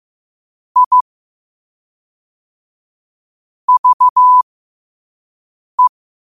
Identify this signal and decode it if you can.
Morse code, heard as a keyed tone, about 15 wpm, spaced as for 5 wpm, 1000 Hz. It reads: I VE
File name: qotdfiles / 5wpm